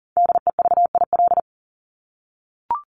Morse code, heard as a keyed tone, about 40 wpm, 700 Hz. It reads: DEVIL
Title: Top 1751-2000 Words 40wpm